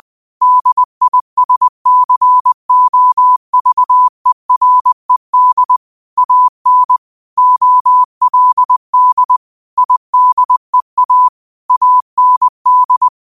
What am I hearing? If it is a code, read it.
Morse code, heard as a keyed tone, about 20 wpm, 1000 Hz. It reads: DISCOVERED AN OLD IDEA AND
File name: qotdfiles / 20wpm